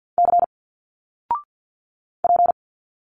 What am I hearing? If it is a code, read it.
Morse code, heard as a keyed tone, about 50 wpm, 700 Hz. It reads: C P
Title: Single Letters 50wpm